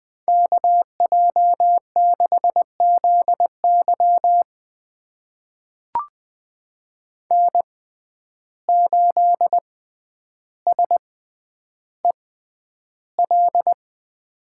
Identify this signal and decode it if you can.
Morse code, heard as a keyed tone, about 20 wpm, spaced as for 10 wpm, 700 Hz. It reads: KJ6ZY N8SEL
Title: USA Call Signs - 5 Characters - 6x Character Spacing - 20wpm